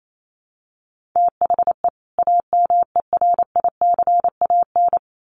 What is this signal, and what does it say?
Morse code, heard as a keyed tone, about 28 wpm, 700 Hz. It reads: THE AMERICAN
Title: Sets of 2 Words 28wpm